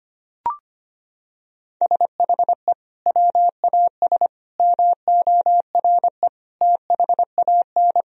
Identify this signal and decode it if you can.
Morse code, heard as a keyed tone, about 25 wpm, 700 Hz. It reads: SHE WAS MORE THAN
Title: Sentences from Top 1000 Words - Review Part 3 - 25wpm